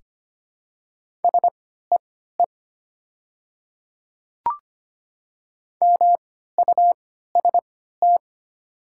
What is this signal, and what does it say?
Morse code, heard as a keyed tone, about 25 wpm, spaced as for 14 wpm, 690 Hz. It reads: SEE MUST
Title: 3x Character Spacing - Top 300 Words - 25wpm